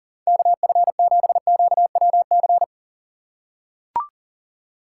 Morse code, heard as a keyed tone, about 37 wpm, 700 Hz. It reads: KF7QWC
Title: USA Call Signs - 4x Character Spacing - 40wpm